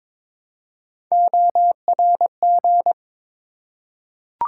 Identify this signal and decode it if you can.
Morse code, heard as a keyed tone, about 22 wpm, 700 Hz. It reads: ORG
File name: ARRL Sections v2 - 22wpm